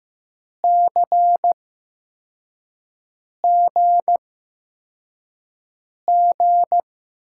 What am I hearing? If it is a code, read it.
Morse code, heard as a keyed tone, about 15 wpm, spaced as for 4 wpm, 700 Hz. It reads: CGG
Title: USA Call Signs - 6 Characters - 8x Character Spacing - 15wpm